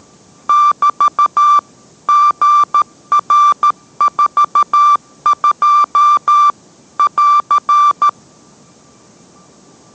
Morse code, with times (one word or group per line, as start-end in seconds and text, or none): 0.5-1.6 <BT>
2.1-6.5 GR42
7.0-8.1 <AR>